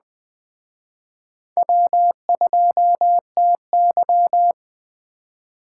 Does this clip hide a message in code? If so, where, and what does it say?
Yes: Morse code from 1.6 to 4.5 s, heard as a keyed tone, about 20 wpm, 700 Hz. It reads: W2TY